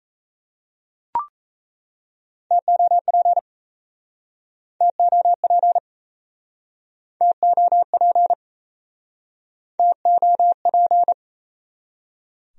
Morse code, heard as a keyed tone, about 42 wpm, 700 Hz. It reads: TOP TOP TOP TOP